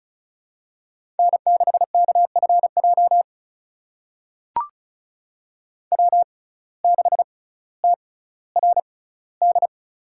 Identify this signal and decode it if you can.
Morse code, heard as a keyed tone, about 36 wpm, spaced as for 18 wpm, 700 Hz. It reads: N6KFJ W6TRD
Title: USA Call Signs - 5 Characters - 6x Character Spacing - 35wpm